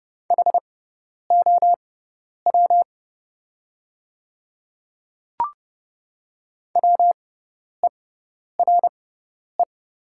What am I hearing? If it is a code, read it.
Morse code, heard as a keyed tone, about 30 wpm, spaced as for 10 wpm, 700 Hz. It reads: HOW WERE
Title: 6x Character Spacing - Top 100 Words - 30wpm